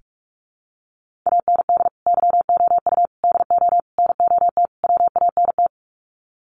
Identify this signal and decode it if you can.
Morse code, heard as a keyed tone, about 44 wpm, 700 Hz. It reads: AND YOU DO NOT WANT
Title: Sets of 5 Words Encore2 45wpm